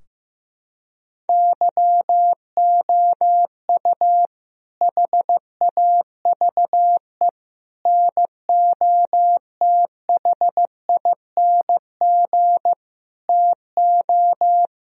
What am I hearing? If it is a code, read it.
Morse code, heard as a keyed tone, about 15 wpm, 700 Hz. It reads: YOU HAVE NOTHING TO
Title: Sentences from Top 1000 Words - Review Part 4 - 15wpm